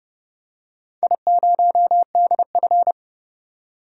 Morse code, heard as a keyed tone, about 30 wpm, 700 Hz. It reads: I0DF
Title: DXCC-100 Call Signs - 4 Characters - 30wpm